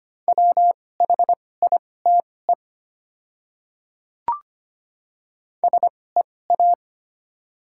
Morse code, heard as a keyed tone, about 25 wpm, spaced as for 18 wpm, 700 Hz. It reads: WHITE SEA
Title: 2x Character Spacing - Top 300 Words - 25wpm